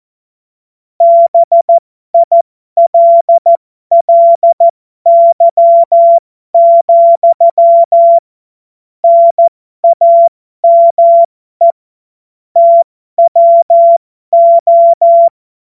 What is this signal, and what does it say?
Morse code, heard as a keyed tone, about 14 wpm, spaced as for 12 wpm, 680 Hz. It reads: BILLY, NAME TWO
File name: Morsekurs / Witze